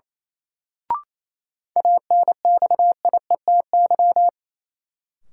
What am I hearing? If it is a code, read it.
Morse code, heard as a keyed tone, about 28 wpm, 700 Hz. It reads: ANXIETY
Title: Sentences from Top 1600 Words - New Word Drill Alt - 28wpm